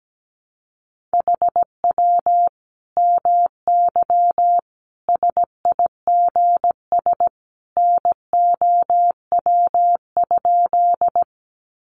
Morse code, heard as a keyed tone, about 17 wpm, 700 Hz. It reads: HW MY SIGS NOW?
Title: Common QSO Phrases 17wpm